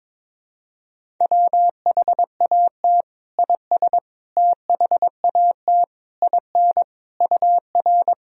Morse code, heard as a keyed tone, about 22 wpm, 700 Hz. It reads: WHAT IS THAT IN UR
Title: Sentences from Top 200 Words and Abbreviations 22wpm